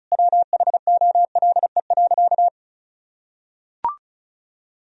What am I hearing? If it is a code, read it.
Morse code, heard as a keyed tone, about 36 wpm, 700 Hz. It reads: WHOLE.
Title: Sentences from Top 1000 Words - Review Part 1 - 35wpm